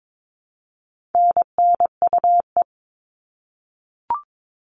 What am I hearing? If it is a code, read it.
Morse code, heard as a keyed tone, about 22 wpm, 700 Hz. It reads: NNUE
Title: All Cut Numbers - 4-Digit 22wpm